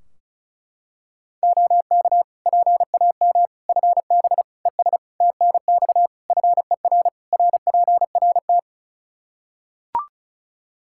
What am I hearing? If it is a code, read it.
Morse code, heard as a keyed tone, about 34 wpm, 700 Hz. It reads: OK PAM FB ES TNX FER RPRT
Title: K1USN SST QSO Practice 35wpm